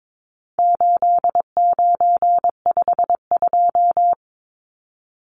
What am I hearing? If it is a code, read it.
Morse code, heard as a keyed tone, about 22 wpm, 700 Hz. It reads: 8952
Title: Numbers - 4-Digit 22wpm